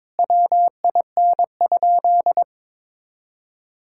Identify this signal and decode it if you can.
Morse code, heard as a keyed tone, about 22 wpm, 700 Hz. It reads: WIN?
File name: practice / 22wpm / Sentences with 4-Word Phrases - 22wpm